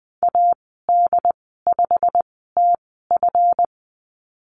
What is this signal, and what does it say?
Morse code, heard as a keyed tone, about 20 wpm, spaced as for 15 wpm, 700 Hz. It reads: AD5TF
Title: 140 Lesson 23g - Letter f - Call Signs 20wpm